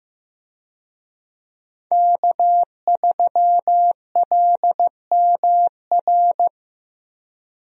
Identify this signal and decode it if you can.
Morse code, heard as a keyed tone, about 15 wpm, 700 Hz. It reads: K3LMR
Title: USA Call Signs - 5 Characters - Repeated 2x - 15wpm